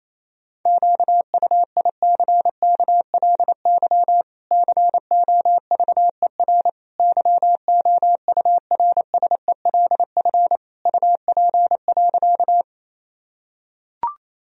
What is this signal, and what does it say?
Morse code, heard as a keyed tone, about 28 wpm, 700 Hz. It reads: QUICKLY COVER YOURSELF UP.
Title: Sentences from Top 1000 Words - Review Part 1 - 28wpm